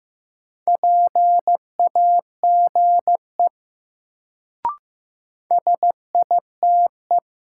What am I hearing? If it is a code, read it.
Morse code, heard as a keyed tone, about 15 wpm, 700 Hz. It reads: PAGE SITE
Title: Sentences from Top 1000 Words - Review Part 1 - New Word Drill - 15wpm